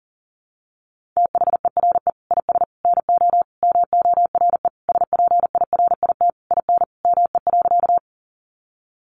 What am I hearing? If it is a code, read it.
Morse code, heard as a keyed tone, about 40 wpm, 700 Hz. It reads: THERE IS NO MORE SPIRIT IN ME.
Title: Sentences from Top 1400 Words 40wpm